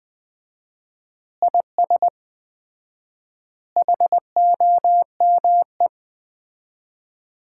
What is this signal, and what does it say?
Morse code, heard as a keyed tone, about 21 wpm, 700 Hz. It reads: IS HOME
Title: Common QSO Phrases - 4x Word-Spacing - 20wpm